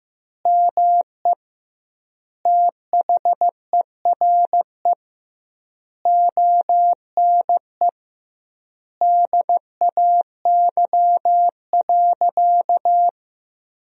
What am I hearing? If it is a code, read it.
Morse code, heard as a keyed tone, about 15 wpm, 700 Hz. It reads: ME THERE ONE DAY.
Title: 2x Word-Spacing - Sentences from Top 300 Words - 15wpm